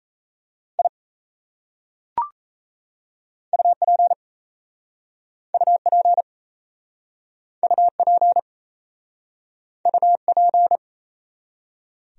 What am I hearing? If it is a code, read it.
Morse code, heard as a keyed tone, about 34 wpm, 700 Hz. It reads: I UP UP UP UP
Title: Speed-Racing Top 100 Words 28wpm